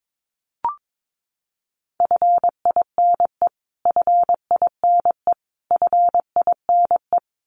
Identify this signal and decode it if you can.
Morse code, heard as a keyed tone, about 22 wpm, 700 Hz. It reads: FINE FINE FINE
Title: Top 500 Words Repeated 22wpm